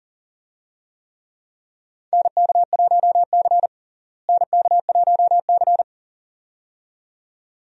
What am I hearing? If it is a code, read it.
Morse code, heard as a keyed tone, about 38 wpm, 700 Hz. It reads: NK1C NK1C
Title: USA Call Signs - 4 Characters - Repeated 2x - 40wpm